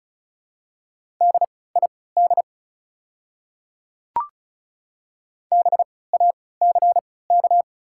Morse code, heard as a keyed tone, about 36 wpm, spaced as for 20 wpm, 690 Hz. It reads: DID BACK
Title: 3x Character Spacing - Top 200 Words - 35wpm